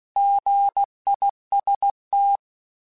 Morse code, heard as a keyed tone, about 16 wpm, 800 Hz. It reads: GIST